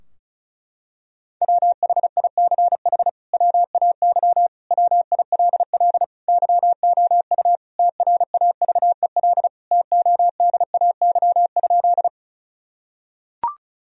Morse code, heard as a keyed tone, about 36 wpm, 700 Hz. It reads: WHICH WAY WILL YOU TRAVEL TODAY?
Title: Sentences from Top 1000 Words - Review Part 2 - 35wpm